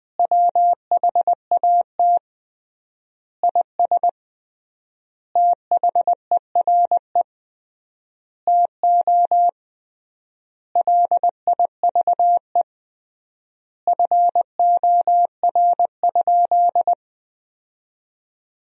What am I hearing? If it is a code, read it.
Morse code, heard as a keyed tone, about 20 wpm, 700 Hz. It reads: WHAT IS THERE TO LIVE FOR?
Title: 3x Word-Spacing - Sentences from Top 200 Words - 20wpm